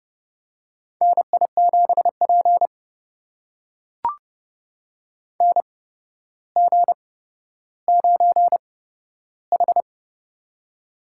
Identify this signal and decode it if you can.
Morse code, heard as a keyed tone, about 30 wpm, spaced as for 13 wpm, 700 Hz. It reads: NI7P NG9H
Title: USA Call Signs - 4 Characters - 8x Character Spacing - 30wpm